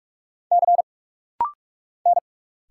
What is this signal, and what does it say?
Morse code, heard as a keyed tone, about 46 wpm, 700 Hz. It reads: C N
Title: Single Letters - Rapid-Fire 45wpm